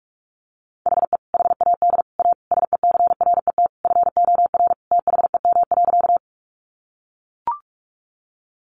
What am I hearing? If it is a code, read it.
Morse code, heard as a keyed tone, about 44 wpm, 700 Hz. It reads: HE HAD A SECRET FOR THEM.
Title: Sentences from Top 1200 Words 45wpm